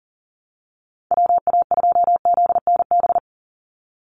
Morse code, heard as a keyed tone, about 37 wpm, 700 Hz. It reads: WA2ZNB
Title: USA Call Signs 40wpm